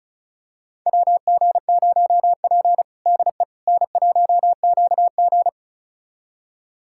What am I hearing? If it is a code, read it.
Morse code, heard as a keyed tone, about 34 wpm, 700 Hz. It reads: WG0P DE N1QG
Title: USA Callsign DE Callsign - 4 Characters - 35wpm